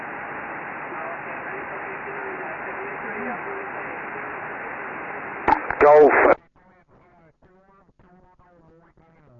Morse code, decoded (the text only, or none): none